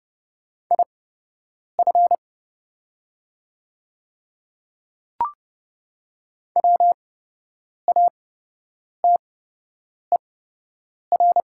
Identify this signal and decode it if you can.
Morse code, heard as a keyed tone, about 29 wpm, spaced as for 8 wpm, 700 Hz. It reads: IF WATER
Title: 8x Character Spacing - Top 100 Words - 30wpm